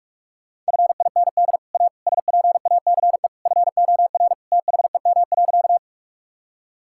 Morse code, heard as a keyed tone, about 42 wpm, 700 Hz. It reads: FIND A SPACE FOR THEM.
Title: Sentences from Top 1000 Words - Review Part 5 - 45wpm